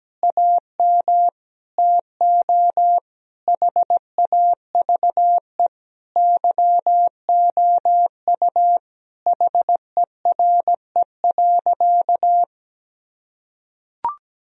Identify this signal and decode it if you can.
Morse code, heard as a keyed tone, about 17 wpm, 700 Hz. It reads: AM TO HAVE YOU HERE.
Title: Sentences from Top 1000 Words - Review Part 2 - 17wpm